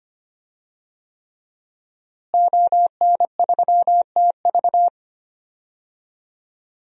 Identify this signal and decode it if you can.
Morse code, heard as a keyed tone, about 25 wpm, 700 Hz. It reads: ON3TV
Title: DXCC-100 Call Signs - 5 Characters - Repeated 3x - 25wpm